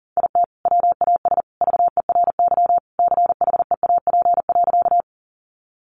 Morse code, heard as a keyed tone, about 40 wpm, 700 Hz. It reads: IT WAS VERY CHEAP.
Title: Sentences from Top 1200 Words 40wpm